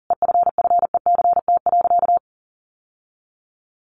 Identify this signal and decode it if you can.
Morse code, heard as a keyed tone, about 40 wpm, 700 Hz. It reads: EFFECT.
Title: Sentences from Top 1400 Words 40wpm